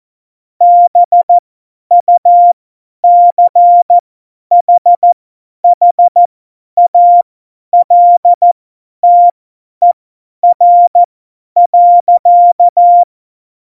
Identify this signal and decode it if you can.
Morse code, heard as a keyed tone, about 14 wpm, spaced as for 10 wpm, 700 Hz. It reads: BUCHHALTER.
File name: Morsekurs / Witze / Witz 26 Mai 2020, 50 F70